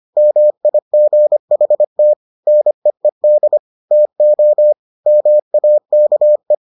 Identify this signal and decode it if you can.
Morse code, heard as a keyed tone, about 25 wpm, 600 Hz. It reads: MIGHT NEED TO MAKE